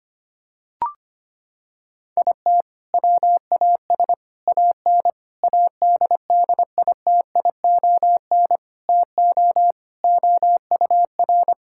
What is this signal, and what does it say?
Morse code, heard as a keyed tone, about 25 wpm, 700 Hz. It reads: IT WAS AN ADDITION TO OUR
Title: Sentences from Top 1000 Words - Review Part 6 - 25wpm